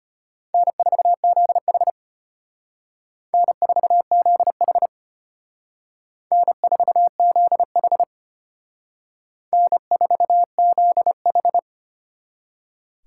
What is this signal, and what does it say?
Morse code, heard as a keyed tone, about 31 wpm, 700 Hz. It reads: N4ZH N4ZH N4ZH N4ZH
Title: Speed Racing - USA Call Signs 25wpm